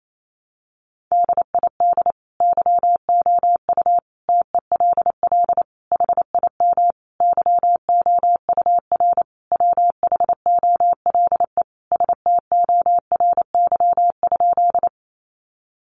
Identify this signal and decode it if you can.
Morse code, heard as a keyed tone, about 28 wpm, 700 Hz. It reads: DID YOU TELL HIM YOUR WHOLE STORY?
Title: Sentences from Top 400 Words 28wpm